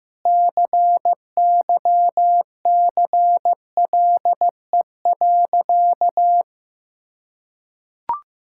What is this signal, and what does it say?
Morse code, heard as a keyed tone, about 16 wpm, 700 Hz. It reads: CYCLE.
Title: Sentences from Top 1000 Words - Review Part 6 - 15wpm